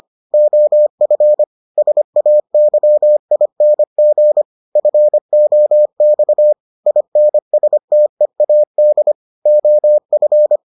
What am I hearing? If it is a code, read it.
Morse code, heard as a keyed tone, about 25 wpm, 600 Hz. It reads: OF SAYING FOX INSTEAD OF